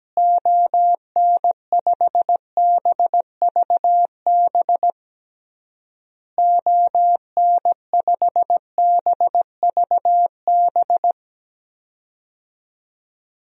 Morse code, heard as a keyed tone, about 17 wpm, 700 Hz. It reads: ON5BVB ON5BVB
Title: DXCC-100 Call Signs - 6 Characters - Repeated 3x - 17wpm